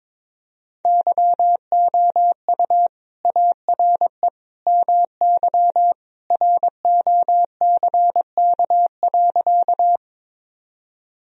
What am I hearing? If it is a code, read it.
Morse code, heard as a keyed tone, about 22 wpm, 700 Hz. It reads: YOU ARE MY ROCK.
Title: Sentences from Top 1000 Words - Review Part 2 - 22wpm